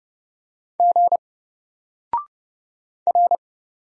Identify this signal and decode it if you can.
Morse code, heard as a keyed tone, about 29 wpm, 700 Hz. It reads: G R
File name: Single-Letter-Number-30wpm